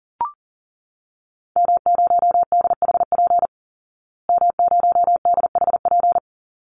Morse code, heard as a keyed tone, about 40 wpm, 700 Hz. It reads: M0DHP M0DHP
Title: Call Signs Repeated 2x - 40wpm